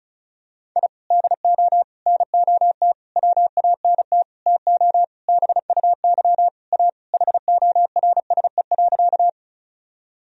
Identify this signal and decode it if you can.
Morse code, heard as a keyed tone, about 34 wpm, 700 Hz. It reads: I DO NOT WANT TO BUY A HORSE.
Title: Sentences from Top 1000 Words - Review Part 4 - 35wpm